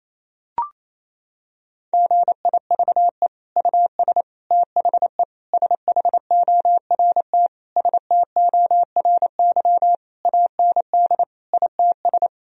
Morse code, heard as a keyed tone, about 28 wpm, 700 Hz. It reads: GIVE US THE SHORT STORY AND ITS
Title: Sentences from Top 1000 Words - Review Part 4 - 28wpm